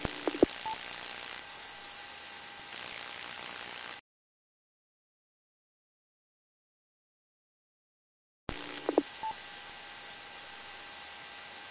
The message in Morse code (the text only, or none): none